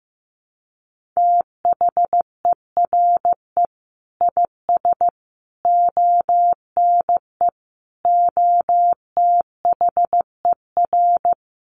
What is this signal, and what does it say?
Morse code, heard as a keyed tone, about 15 wpm, 700 Hz. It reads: THERE IS ONE OTHER
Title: Sentences from Top 1500 Words 15wpm